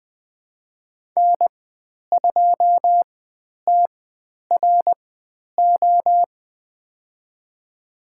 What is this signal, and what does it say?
Morse code, heard as a keyed tone, about 20 wpm, spaced as for 10 wpm, 700 Hz. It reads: N2TRO
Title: 095 Lesson 16g - Letter c - Call Signs 20wpm